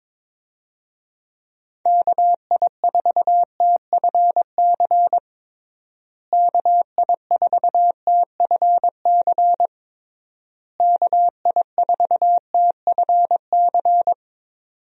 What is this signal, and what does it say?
Morse code, heard as a keyed tone, about 22 wpm, 700 Hz. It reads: KI4TFC KI4TFC KI4TFC
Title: USA Call Signs - 6 Characters - Repeated 3x - 22wpm